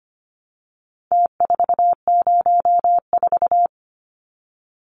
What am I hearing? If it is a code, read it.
Morse code, heard as a keyed tone, about 25 wpm, 700 Hz. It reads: T404
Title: Numbers - Contest Style - 4-Digit 25wpm